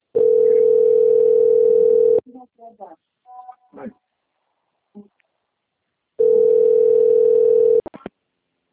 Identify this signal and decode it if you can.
Ringback tone (North America), 440 + 480 Hz, with speech.
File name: out- - - - - .43971